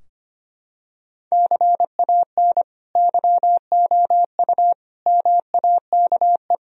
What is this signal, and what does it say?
Morse code, heard as a keyed tone, about 25 wpm, 700 Hz. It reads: CAN YOU MAKE
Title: Sentences from Top 1000 Words - Review Part 4 - 25wpm